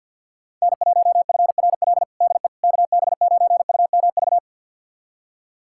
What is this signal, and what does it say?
Morse code, heard as a keyed tone, about 46 wpm, 700 Hz. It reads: N1FRL DE KB9UMV
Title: USA Callsign DE Callsign 50wpm